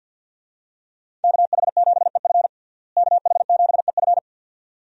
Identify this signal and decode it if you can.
Morse code, heard as a keyed tone, about 51 wpm, 700 Hz. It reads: KH7EF KH7EF
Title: USA Call Signs - 5 Characters - Repeated 2x - 50wpm